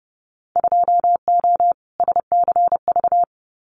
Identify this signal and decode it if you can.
Morse code, heard as a keyed tone, about 30 wpm, 700 Hz. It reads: 2O SCV
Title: Winter Field Day Exchanges Encore 30wpm